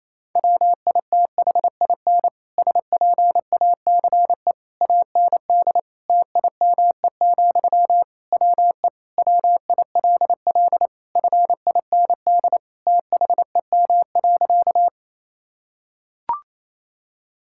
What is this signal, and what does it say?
Morse code, heard as a keyed tone, about 28 wpm, 700 Hz. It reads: WITHIN SPACE AND TIME, WE WILL FIND THEM.
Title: Sentences from Top 1000 Words - Review Part 5 - 28wpm